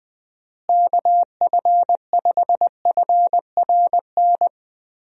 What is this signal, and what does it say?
Morse code, heard as a keyed tone, about 20 wpm, 700 Hz. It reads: KF5FRN
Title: USA Call Signs - 6 Characters - Repeated 2x - 20wpm